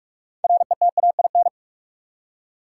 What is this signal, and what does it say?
Morse code, heard as a keyed tone, about 47 wpm, 700 Hz. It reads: RETAIN
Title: Six Letter Words - 45wpm